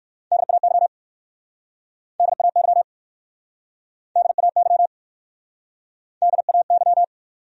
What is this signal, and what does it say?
Morse code, heard as a keyed tone, about 59 wpm, 700 Hz. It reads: DAY DAY DAY DAY